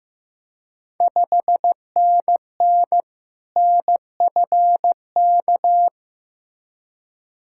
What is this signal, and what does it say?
Morse code, heard as a keyed tone, about 15 wpm, 700 Hz. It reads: 5NN NFK
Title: Ontario QSO Party Exchange 15wpm